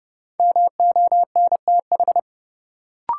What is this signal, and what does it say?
Morse code, heard as a keyed tone, about 30 wpm, 700 Hz. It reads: MONTH
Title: Sentences from Top 800 Words - New Word Drill - 30wpm